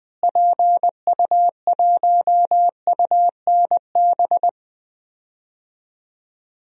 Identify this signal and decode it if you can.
Morse code, heard as a keyed tone, about 21 wpm, 700 Hz. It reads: PU1UNB
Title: DXCC-100 Call Signs - 6 Characters - Repeated 2x - 20wpm